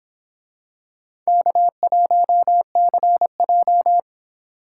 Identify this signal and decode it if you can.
Morse code, heard as a keyed tone, about 26 wpm, 700 Hz. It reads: K1CJ